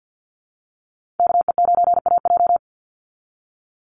Morse code, heard as a keyed tone, about 50 wpm, 700 Hz. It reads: KE9AJ
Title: NA SOTA Chaser Callsigns 50wpm